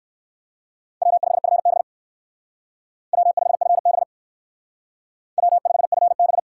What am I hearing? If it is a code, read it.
Morse code, heard as a keyed tone, about 68 wpm, 700 Hz. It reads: W5FB W5FB W5FB